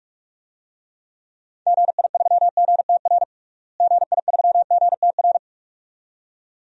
Morse code, heard as a keyed tone, about 43 wpm, 700 Hz. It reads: GI3GTR GI3GTR
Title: DXCC-100 Call Signs - 6 Characters - Repeated 2x - 45wpm